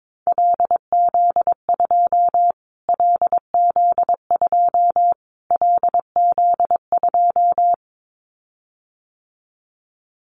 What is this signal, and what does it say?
Morse code, heard as a keyed tone, about 22 wpm, 700 Hz. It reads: LZ2 LZ2 LZ2
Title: Callsign Prefixes for Top 25 DXCC - Repeated 3x - 22wpm